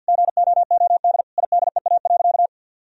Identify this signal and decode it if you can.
Morse code, heard as a keyed tone, about 50 wpm, 700 Hz. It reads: GOOD IDEA.
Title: Sentences from Top 1000 Words - Review Part 4 - 50wpm